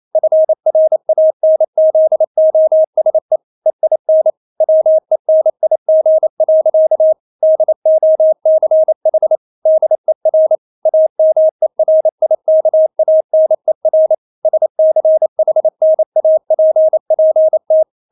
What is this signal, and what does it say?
Morse code, heard as a keyed tone, about 28 wpm, 630 Hz. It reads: FRANZOSE EIN WENIG. DOCH DER AMERIKANER SCHNAPPT
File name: Witz9 28wpm